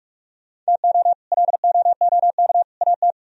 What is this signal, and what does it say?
Morse code, heard as a keyed tone, about 63 wpm, spaced as for 54 wpm, 700 Hz. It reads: TO LOOK AT
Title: Speed-Racing - Sets of 3 Words 45wpm